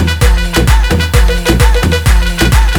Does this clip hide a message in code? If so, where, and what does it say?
No.